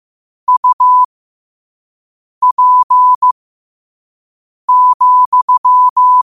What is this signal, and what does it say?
Morse code, heard as a keyed tone, about 15 wpm, spaced as for 5 wpm, 1000 Hz. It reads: UP,
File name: qotdfiles / 5wpm